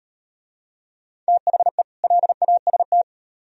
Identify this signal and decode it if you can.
Morse code, heard as a keyed tone, about 38 wpm, 700 Hz. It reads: THE LAST